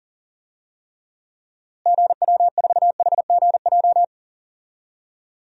Morse code, heard as a keyed tone, about 40 wpm, 700 Hz. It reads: GW4HGJ
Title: DXCC-100 Call Signs - 6 Characters - Repeated 2x - 40wpm